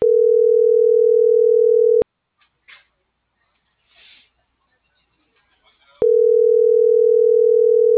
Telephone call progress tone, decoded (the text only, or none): ringback tone